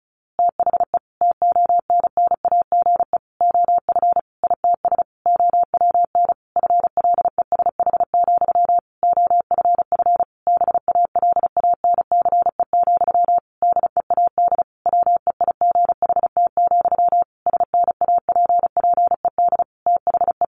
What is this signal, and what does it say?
Morse code, heard as a keyed tone, about 34 wpm, 700 Hz. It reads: THE TONNAGE OF ITS OWN FLESH, OFF BALANCE, DEAD WEIGHT, SNAPPED THE
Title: A Sound Of Thunder 35wpm